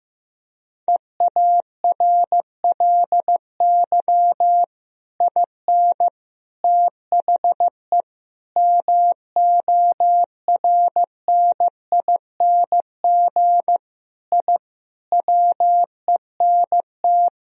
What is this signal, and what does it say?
Morse code, heard as a keyed tone, about 15 wpm, 700 Hz. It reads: EARLY IN THE MORNING I WENT
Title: Sentences from Top 1000 Words - Review Part 2 - 15wpm